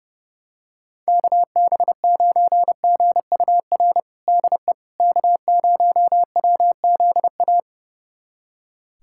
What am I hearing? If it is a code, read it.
Morse code, heard as a keyed tone, about 30 wpm, 700 Hz. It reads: KB9GUR DE K0WZA
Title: USA Callsign DE Callsign 30wpm